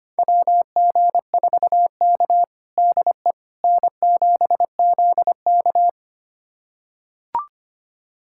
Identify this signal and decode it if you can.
Morse code, heard as a keyed tone, about 25 wpm, 700 Hz. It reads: WG4K DE N7ZK
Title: USA Callsign DE Callsign - 4 Characters - 25wpm